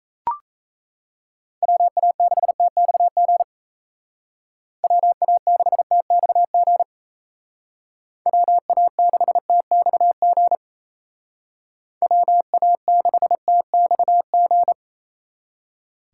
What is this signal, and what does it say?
Morse code, heard as a keyed tone, about 34 wpm, 700 Hz. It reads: WA6TXG WA6TXG WA6TXG WA6TXG
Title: Speed Racing - USA 2x3 Call Signs 28wpm